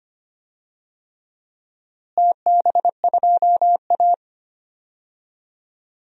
Morse code, heard as a keyed tone, about 25 wpm, 700 Hz. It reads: TB2A
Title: DXCC-100 Call Signs - 4 Characters - Repeated 3x - 25wpm